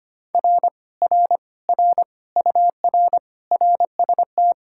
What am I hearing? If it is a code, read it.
Morse code, heard as a keyed tone, about 25 wpm, 700 Hz. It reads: R R R UR RST
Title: Speed-Racing Common QSO Phrases Encore - 25wpm